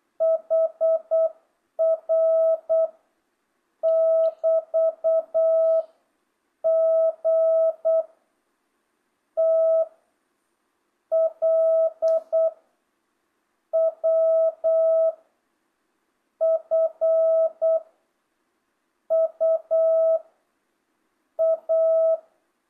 Morse code, heard as a keyed tone, about 8 wpm, 660 Hz. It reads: HR <BT> G T L W F U A
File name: jh8cbh / 2025cw